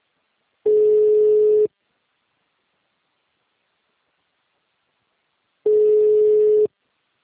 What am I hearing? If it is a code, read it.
Ringback tone (Europe, 425 Hz).